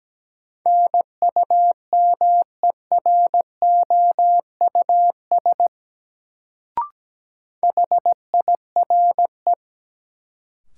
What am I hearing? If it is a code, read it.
Morse code, heard as a keyed tone, about 17 wpm, 700 Hz. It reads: NUMEROUS HIRE
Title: Sentences from Top 1400 Words - New Word Drill Alt - 17wpm